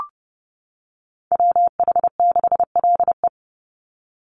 Morse code, heard as a keyed tone, about 30 wpm, 700 Hz. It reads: WH6LE
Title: NA SOTA Chaser Callsigns 30wpm